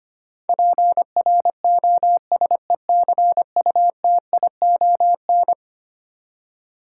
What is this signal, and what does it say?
Morse code, heard as a keyed tone, about 25 wpm, 700 Hz. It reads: PROSECUTION
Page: morse-code-ninja-practice-25wpm